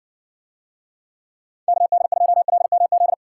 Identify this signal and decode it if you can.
Morse code, heard as a keyed tone, about 60 wpm, 700 Hz. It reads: KD2LMZ